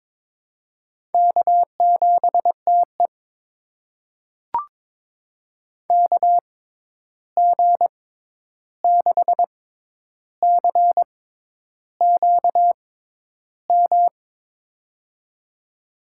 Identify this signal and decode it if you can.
Morse code, heard as a keyed tone, about 22 wpm, spaced as for 10 wpm, 700 Hz. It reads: K7TE KG6CQM
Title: USA Call Signs - 6x Character Spacing - 22wpm